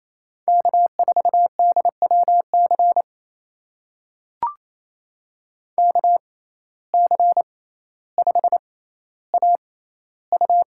Morse code, heard as a keyed tone, about 28 wpm, spaced as for 14 wpm, 700 Hz. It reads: K4DWC KC5AU
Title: USA Call Signs - 5 Characters - 6x Character Spacing - 28wpm